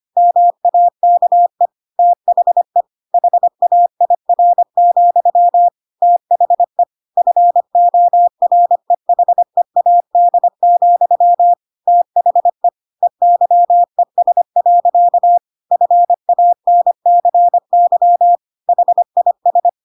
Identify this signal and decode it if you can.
Morse code, heard as a keyed tone, about 25 wpm, 700 Hz. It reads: MAKE THE HAIR, THE FOREHEAD, THE EYES. FANCY HIS
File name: Pinocchio / Pinocchio ch3 25 25 ik1hge